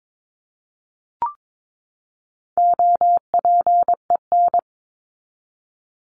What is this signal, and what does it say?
Morse code, heard as a keyed tone, about 22 wpm, 700 Hz. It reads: OPEN
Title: Top 300 Words 22wpm